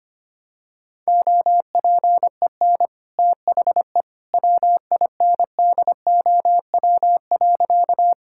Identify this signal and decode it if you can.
Morse code, heard as a keyed tone, about 25 wpm, 700 Hz. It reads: OPEN THE WINDOW.
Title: Sentences from Top 1600 Words 25wpm